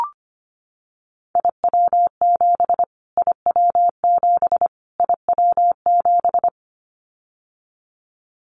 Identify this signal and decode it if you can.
Morse code, heard as a keyed tone, about 25 wpm, 700 Hz. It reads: IW7 IW7 IW7
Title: Callsign Prefixes for Top 25 DXCC - Repeated 3x - 25wpm